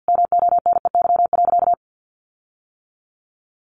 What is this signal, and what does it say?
Morse code, heard as a keyed tone, about 50 wpm, 700 Hz. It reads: MONEY.
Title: Sentences from Top 1500 Words 50wpm